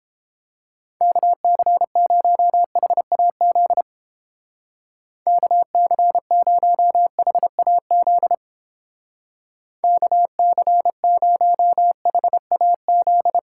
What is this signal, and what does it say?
Morse code, heard as a keyed tone, about 29 wpm, 700 Hz. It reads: KC0HAZ KC0HAZ KC0HAZ